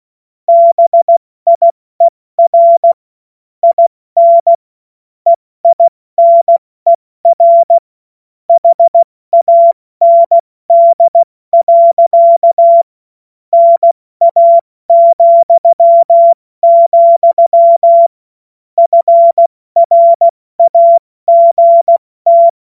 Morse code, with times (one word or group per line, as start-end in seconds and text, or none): 0.5-2.9 BIER
3.6-4.5 IN
5.3-7.8 EINER
8.5-12.8 HAND.
13.5-18.1 NA,,
18.8-22.5 FRAGT